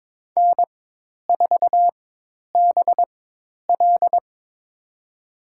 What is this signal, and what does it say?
Morse code, heard as a keyed tone, about 22 wpm, spaced as for 10 wpm, 700 Hz. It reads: N4BL
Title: USA Call Signs - 4 Characters - 4x Character Spacing - 22wpm